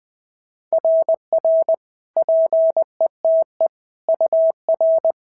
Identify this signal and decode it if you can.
Morse code, heard as a keyed tone, about 18 wpm, 650 Hz. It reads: RR PETE UR
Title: SOTA QSO Practice 20wpm